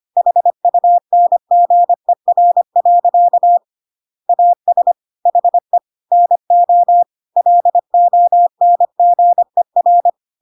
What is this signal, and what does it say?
Morse code, heard as a keyed tone, about 25 wpm, 700 Hz. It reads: HUNGER. AS HE NO LONGER